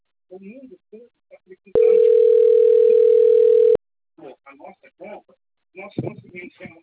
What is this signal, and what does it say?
Ringback tone (North America), 440 + 480 Hz, with speech.
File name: out- - - - - .55321